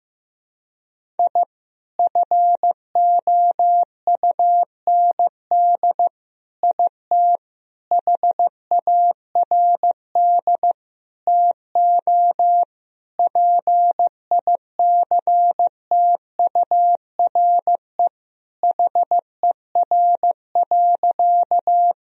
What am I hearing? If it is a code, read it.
Morse code, heard as a keyed tone, about 15 wpm, 700 Hz. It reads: I FOUND IT HARD TO PICTURE HER.
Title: Sentences from Top 1000 Words - Review Part 1 - 15wpm